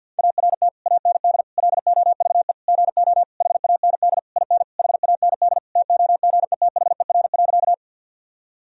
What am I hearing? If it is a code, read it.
Morse code, heard as a keyed tone, about 47 wpm, 700 Hz. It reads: ART AND LOVE GO HAND IN HAND TOGETHER.
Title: Sentences from Top 1000 Words - Review Part 3 - 50wpm